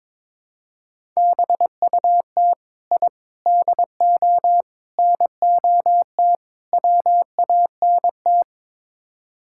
Morse code, heard as a keyed tone, about 22 wpm, 700 Hz. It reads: BUT I DO NOT WANT
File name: practice / 22wpm / Build A Phrase - 5-Word Phrases - 22wpm